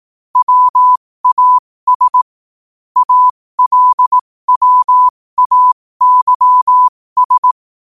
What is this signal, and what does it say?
Morse code, heard as a keyed tone, about 18 wpm, spaced as for 15 wpm, 1000 Hz. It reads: WAS ALWAYS
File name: qotdfiles / 15wpm